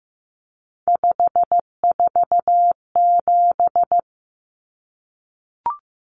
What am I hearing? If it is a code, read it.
Morse code, heard as a keyed tone, about 15 wpm, 700 Hz. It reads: 547
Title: Numbers - 3-Digit 15wpm